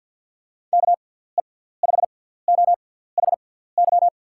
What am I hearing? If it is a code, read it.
Morse code, heard as a keyed tone, about 52 wpm, spaced as for 17 wpm, 700 Hz. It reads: KE5OHY
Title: USA Call Signs - 6 Characters - 6x Character Spacing - 50wpm